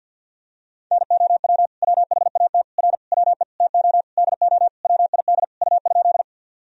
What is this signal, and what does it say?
Morse code, heard as a keyed tone, about 52 wpm, 700 Hz. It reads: NOW WHAT R WE TO DO WID U?
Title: Sentences from Top 100 Words and Abbreviations 50wpm